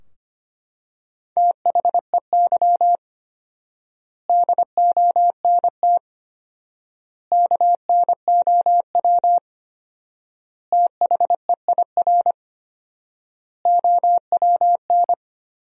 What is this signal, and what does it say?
Morse code, heard as a keyed tone, about 25 wpm, 700 Hz. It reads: THEY DONT KNOW THEIR OWN
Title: 4x Word-Spacing - Sentences from Top 300 Words - 25wpm